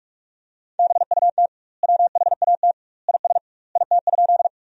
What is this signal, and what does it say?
Morse code, heard as a keyed tone, about 46 wpm, 700 Hz. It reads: BUT WHAT IS IT?
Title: 2x Word-Spacing - Sentences from Top 100 Words - 45wpm